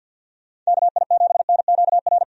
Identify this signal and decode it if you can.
Morse code, heard as a keyed tone, about 50 wpm, 700 Hz. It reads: KI7NQR
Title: USA Call Signs - 6 Characters - 50wpm